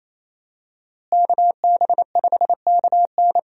Morse code, heard as a keyed tone, about 28 wpm, 700 Hz. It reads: KB5KN
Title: USA Call Signs - 5 Characters - Repeated 2x - 28wpm